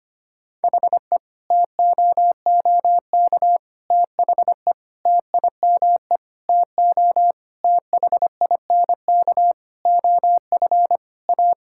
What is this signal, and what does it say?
Morse code, heard as a keyed tone, about 25 wpm, 700 Hz. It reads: HE TOOK THE TIME TO THINK OF A
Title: Sentences from Top 1600 Words 25wpm